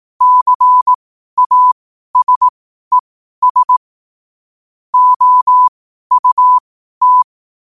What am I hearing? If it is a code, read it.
Morse code, heard as a keyed tone, about 18 wpm, spaced as for 12 wpm, 1000 Hz. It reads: CASES OUT
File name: qotdfiles / 12wpm